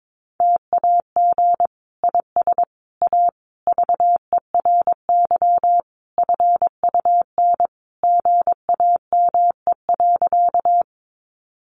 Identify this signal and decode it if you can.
Morse code, heard as a keyed tone, about 22 wpm, 700 Hz. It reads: TAG IS A VERY FUN GAME.
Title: Additional Easy Phrases 22wpm